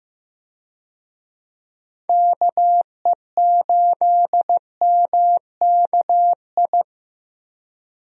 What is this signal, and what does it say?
Morse code, heard as a keyed tone, about 15 wpm, 700 Hz. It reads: KE8MKI
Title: USA Call Signs - 6 Characters - Repeated 3x - 15wpm